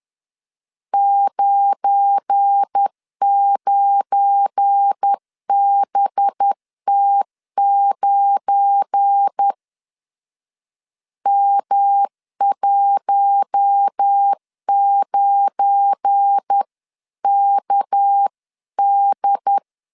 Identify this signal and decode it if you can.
Morse code, heard as a keyed tone, about 10 wpm, 800 Hz. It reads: 99BT9 M19KD
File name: LEZIONI / 7 GF9Z / 7 GF9 10wpm 2747